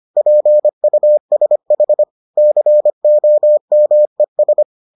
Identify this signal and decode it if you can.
Morse code, heard as a keyed tone, about 25 wpm, 600 Hz. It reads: PUSH COMES